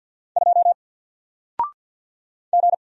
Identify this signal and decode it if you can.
Morse code, heard as a keyed tone, about 53 wpm, 700 Hz. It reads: 2 G
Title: Single Letter-Number 50wpm